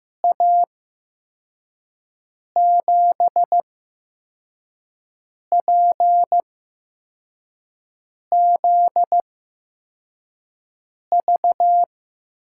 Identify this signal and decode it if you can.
Morse code, heard as a keyed tone, about 15 wpm, spaced as for 4 wpm, 700 Hz. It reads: A7PZV
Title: USA Call Signs - 6 Characters - 8x Character Spacing - 15wpm